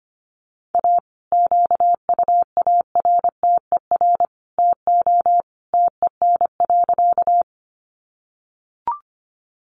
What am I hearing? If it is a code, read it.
Morse code, heard as a keyed tone, about 25 wpm, 700 Hz. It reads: A QUARTER TO TEN.
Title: Sentences from Top 1300 Words 25wpm